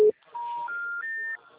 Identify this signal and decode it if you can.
Special information tone, with noise.
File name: out- -2007- - - .13